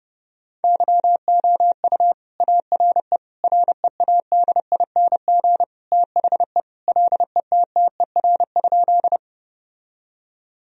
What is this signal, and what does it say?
Morse code, heard as a keyed tone, about 29 wpm, 700 Hz. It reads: YOU ARE READING THE LETTER?
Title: Sentences from Top 1000 Words - Review Part 3 - 30wpm